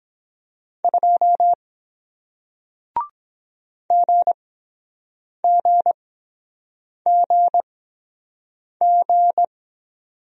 Morse code, heard as a keyed tone, about 23 wpm, spaced as for 7 wpm, 700 Hz. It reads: 2 GGGG